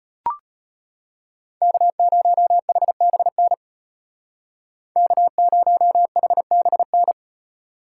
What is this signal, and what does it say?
Morse code, heard as a keyed tone, about 32 wpm, 700 Hz. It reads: K0HBN K0HBN